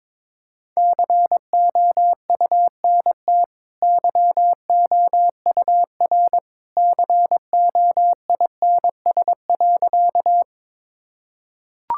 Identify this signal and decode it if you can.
Morse code, heard as a keyed tone, about 21 wpm, 690 Hz. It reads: COUNT YOUR COINS.
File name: practice / 22wpm / Easy Sentences with Hints - 22wpm